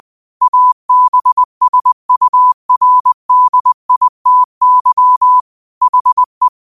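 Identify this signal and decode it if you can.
Morse code, heard as a keyed tone, about 20 wpm, 1000 Hz. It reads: ABSURDITY HE
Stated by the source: Morse code quote of the day at 20 wpm.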